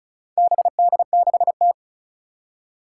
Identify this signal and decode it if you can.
Morse code, heard as a keyed tone, about 36 wpm, 700 Hz. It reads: BD6T
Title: Four Challenging Letter-Number Combinations 35wpm